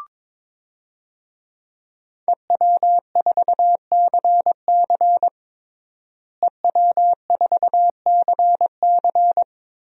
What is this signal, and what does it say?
Morse code, heard as a keyed tone, about 22 wpm, 700 Hz. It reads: EW4CC EW4CC
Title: DXCC-100 Call Signs - 5 Characters - Repeated 3x - 22wpm